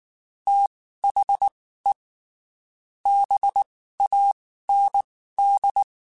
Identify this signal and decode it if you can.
Morse code, heard as a keyed tone, about 19 wpm, spaced as for 14 wpm, 800 Hz. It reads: THE BAND
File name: QSOs / QSOs-mp3s / WD0BGP 1@18